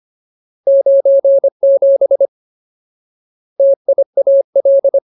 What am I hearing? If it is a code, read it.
Morse code, heard as a keyed tone, about 25 wpm, 550 Hz. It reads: 97 TIAL
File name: Ultimate Word List / 25WPM Ultimate